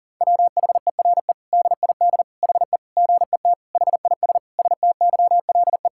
Morse code, heard as a keyed tone, about 40 wpm, 700 Hz. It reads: WHERE DID HE GET HIS STYLE
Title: Sentences from Top 1000 Words - Review Part 5 - 40wpm